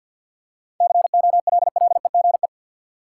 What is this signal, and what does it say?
Morse code, heard as a keyed tone, about 53 wpm, 700 Hz. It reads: COLLEGE
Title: Seven Letter Words - 50wpm